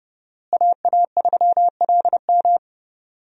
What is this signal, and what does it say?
Morse code, heard as a keyed tone, about 30 wpm, 700 Hz. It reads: AA3LM
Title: USA Call Signs - 5 Characters - Repeated 2x - 30wpm